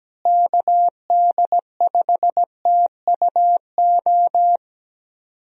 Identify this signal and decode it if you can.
Morse code, heard as a keyed tone, about 17 wpm, 700 Hz. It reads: KD5TUO
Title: USA Call Signs - 6 Characters - Repeated 2x - 17wpm